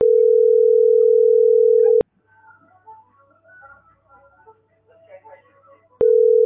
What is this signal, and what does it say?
Ringback tone (North America), 440 + 480 Hz.